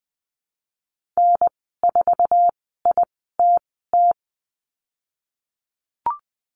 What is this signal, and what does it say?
Morse code, heard as a keyed tone, about 20 wpm, spaced as for 14 wpm, 700 Hz. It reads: N4ITT
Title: 055 Lesson 10g - Letter r - Call Signs 20wpm